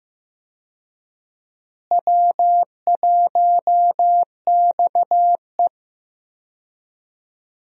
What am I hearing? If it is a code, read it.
Morse code, heard as a keyed tone, about 15 wpm, 700 Hz. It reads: W1XE
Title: USA Call Signs - 4 Characters - Repeated 3x - 15wpm